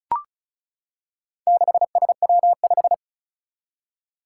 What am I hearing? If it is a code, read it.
Morse code, heard as a keyed tone, about 36 wpm, 700 Hz. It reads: 6SW5
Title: Four Challenging Letter-Number Combinations 35wpm